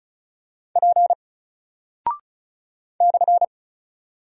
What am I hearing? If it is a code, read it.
Morse code, heard as a keyed tone, about 34 wpm, 700 Hz. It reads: P /
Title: Single Character (Letter:Number:Punctuation:Prosign) Alt 35wpm